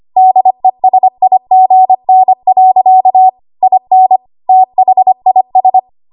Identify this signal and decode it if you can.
Morse code, heard as a keyed tone, about 25 wpm, 750 Hz. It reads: DESIGN. IN THIS
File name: Archive / 25 WPM / 25WPM